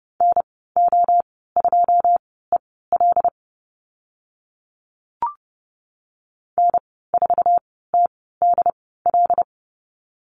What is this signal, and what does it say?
Morse code, heard as a keyed tone, about 30 wpm, spaced as for 17 wpm, 700 Hz. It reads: NO2EL N4TDL
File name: Lesson 15g - Number 5 - Call Signs 30wpm